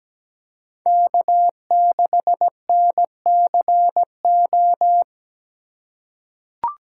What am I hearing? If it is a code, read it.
Morse code, heard as a keyed tone, about 17 wpm, 700 Hz. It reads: K6NCO
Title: USA Call Signs - 5 Characters - 3x Character Spacing - 17wpm